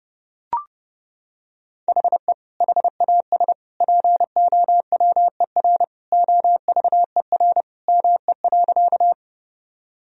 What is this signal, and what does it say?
Morse code, heard as a keyed tone, about 32 wpm, 700 Hz. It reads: HE HAS POWER OVER ME.
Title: Sentences from Top 1000 Words - Review Part 2 - 30wpm